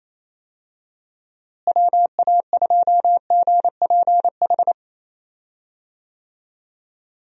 Morse code, heard as a keyed tone, about 28 wpm, 700 Hz. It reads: WA2GPH
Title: USA Call Signs - 6 Characters - Repeated 3x - 28wpm